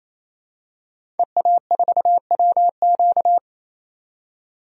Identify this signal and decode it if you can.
Morse code, heard as a keyed tone, about 28 wpm, 700 Hz. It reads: EA4WQ
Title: DXCC-100 Call Signs - 5 Characters - 28wpm